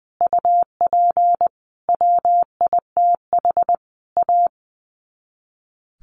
Morse code, heard as a keyed tone, about 20 wpm, 700 Hz. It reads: UP WITH A
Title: Sets of 4 Words 20wpm